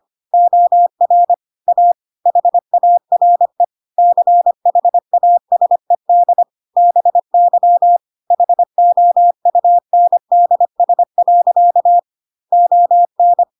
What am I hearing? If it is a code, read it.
Morse code, heard as a keyed tone, about 25 wpm, 700 Hz. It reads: OR A HARE CHASED BY HOUNDS. ON
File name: Pinocchio / Pinocchio ch4 25 25 ik1hge